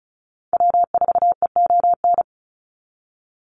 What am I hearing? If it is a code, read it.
Morse code, heard as a keyed tone, about 34 wpm, 700 Hz. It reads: W4EON
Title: NA SOTA Activator Callsigns 35wpm